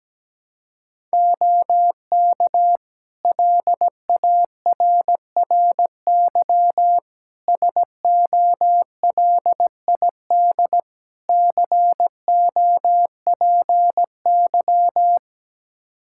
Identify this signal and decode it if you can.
Morse code, heard as a keyed tone, about 17 wpm, 700 Hz. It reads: OK LARRY SOLID COPY
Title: Practice QSO - Exchange 3 - More Info - 17wpm